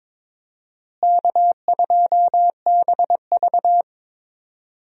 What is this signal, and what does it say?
Morse code, heard as a keyed tone, about 22 wpm, 700 Hz. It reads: K2BV
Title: USA Call Signs - 4 Characters - Repeated 2x - 22wpm